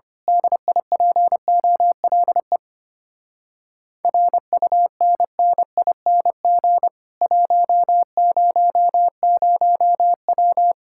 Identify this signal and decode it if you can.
Morse code, heard as a keyed tone, about 25 wpm, 700 Hz. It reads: DIPOLE RUNNING 100W
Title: Speed-Racing Common QSO Phrases Encore - 25wpm